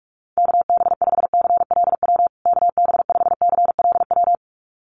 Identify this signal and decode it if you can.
Morse code, heard as a keyed tone, about 44 wpm, 700 Hz. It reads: KB5CLW KB5CLW
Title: USA Call Signs Repeated 2x 45wpm